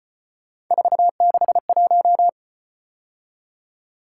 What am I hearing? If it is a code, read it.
Morse code, heard as a keyed tone, about 34 wpm, 700 Hz. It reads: 461